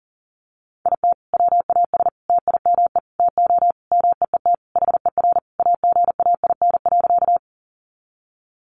Morse code, heard as a keyed tone, about 40 wpm, 700 Hz. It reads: IT WAS TIME TO MEET HER AGAIN.
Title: Sentences from Top 1100 Words 40wpm